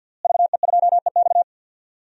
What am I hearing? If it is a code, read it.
Morse code, heard as a keyed tone, about 50 wpm, 700 Hz. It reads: VE2EX
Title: DXCC-100 Call Signs - 5 Characters - Repeated 2x - 50wpm